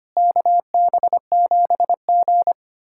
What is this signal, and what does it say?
Morse code, heard as a keyed tone, about 25 wpm, 700 Hz. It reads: KB7G
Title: USA Callsign DE Callsign - 4 Characters - 25wpm